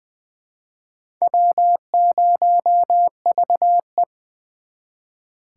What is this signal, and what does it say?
Morse code, heard as a keyed tone, about 25 wpm, 700 Hz. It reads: W0VE
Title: Speed Racing - USA 1x2 Call Signs 20wpm